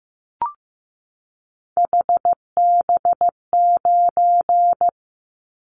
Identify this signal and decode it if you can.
Morse code, heard as a keyed tone, about 15 wpm, 700 Hz. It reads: HB9
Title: Callsign Prefixes for Top 25 DXCC - 15wpm